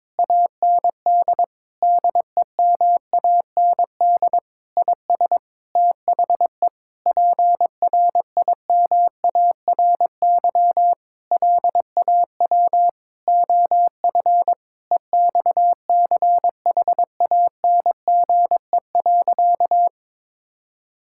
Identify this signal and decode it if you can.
Morse code, heard as a keyed tone, about 22 wpm, 700 Hz. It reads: AND DEMAND IS THE PRIMARY LAW OF EXCHANGE.
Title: Sentences from Top 1000 Words - Review Part 6 - 22wpm